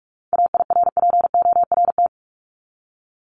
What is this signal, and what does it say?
Morse code, heard as a keyed tone, about 44 wpm, 700 Hz. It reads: AIRPORT
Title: Top 1751-2000 Words 45wpm